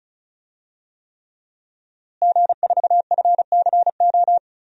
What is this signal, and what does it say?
Morse code, heard as a keyed tone, about 34 wpm, 700 Hz. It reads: G4FCO
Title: DXCC-100 Call Signs - 5 Characters - Repeated 3x - 35wpm